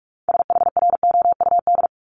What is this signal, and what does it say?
Morse code, heard as a keyed tone, about 46 wpm, 700 Hz. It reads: SHROUD
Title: Multi-letter Phonemes 45wpm